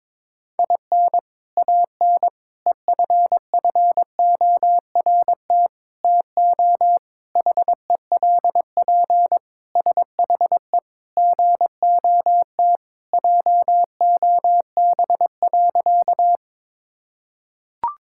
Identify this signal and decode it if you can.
Morse code, heard as a keyed tone, about 21 wpm, 700 Hz. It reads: IN AN EFFORT TO HELP SHE GOT JOB.
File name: practice / 22wpm / Sentences with 4-Word Phrases - 22wpm